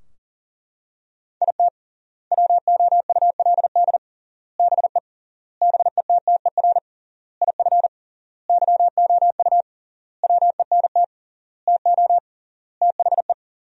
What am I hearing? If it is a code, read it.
Morse code, heard as a keyed tone, about 40 wpm, 700 Hz. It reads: IT WOULD BE BETTER IF YOU WENT TO THE
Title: 3x Word-Spacing - Sentences from Top 400 Words - 40wpm